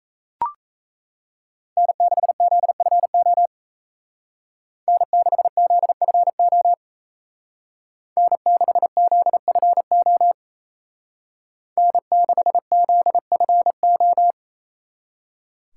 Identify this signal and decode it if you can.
Morse code, heard as a keyed tone, about 33 wpm, 700 Hz. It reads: N6ZFO N6ZFO N6ZFO N6ZFO
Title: Speed Racing - USA 1x3 Call Signs 28wpm